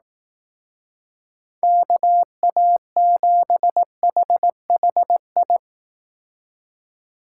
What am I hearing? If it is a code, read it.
Morse code, heard as a keyed tone, about 18 wpm, 700 Hz. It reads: KA7HHI
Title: Speed Racing - USA Call Signs 15wpm